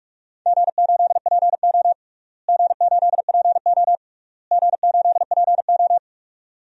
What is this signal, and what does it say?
Morse code, heard as a keyed tone, about 45 wpm, 700 Hz. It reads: G8PO G8PO G8PO
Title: DXCC-100 Call Signs - 4 Characters - Repeated 3x - 45wpm